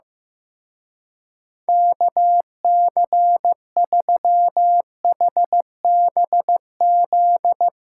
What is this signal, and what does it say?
Morse code, heard as a keyed tone, about 16 wpm, 700 Hz. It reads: KC3HBZ
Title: Speed Racing - USA Call Signs 15wpm